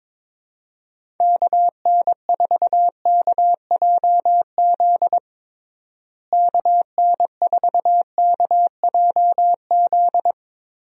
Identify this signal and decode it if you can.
Morse code, heard as a keyed tone, about 22 wpm, 700 Hz. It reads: KN4KJZ KN4KJZ
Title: USA Call Signs - 6 Characters - Repeated 2x - 22wpm